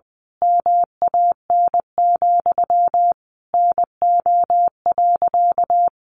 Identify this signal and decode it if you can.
Morse code, heard as a keyed tone, about 20 wpm, 700 Hz. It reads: MAN, NO.
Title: A Sound Of Thunder 20wpm